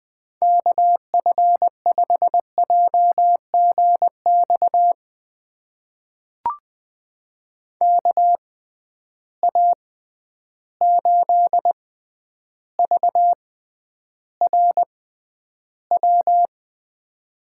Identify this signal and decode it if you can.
Morse code, heard as a keyed tone, about 21 wpm, spaced as for 10 wpm, 700 Hz. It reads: KF5JGX KA8VRW
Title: USA Call Signs - 6 Characters - 6x Character Spacing - 20wpm